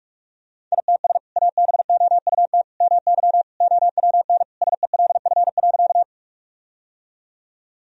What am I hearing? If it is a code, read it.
Morse code, heard as a keyed tone, about 46 wpm, 700 Hz. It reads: ITS ABOUT MY OWN SELF.
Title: Sentences from Top 1000 Words - Review Part 1 - 45wpm